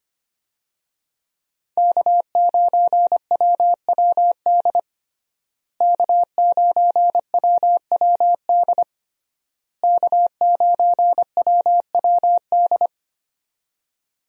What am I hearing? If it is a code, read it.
Morse code, heard as a keyed tone, about 25 wpm, 700 Hz. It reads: K9WWD K9WWD K9WWD
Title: USA Call Signs - 5 Characters - Repeated 3x - 25wpm